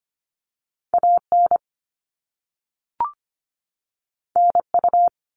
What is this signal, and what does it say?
Morse code, heard as a keyed tone, about 25 wpm, 700 Hz. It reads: AN NU
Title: All Cut Numbers - 2-Digit 25wpm